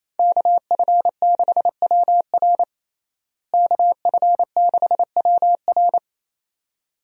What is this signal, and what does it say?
Morse code, heard as a keyed tone, about 28 wpm, 700 Hz. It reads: KF6WR KF6WR
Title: USA Call Signs - 5 Characters - Repeated 2x - 28wpm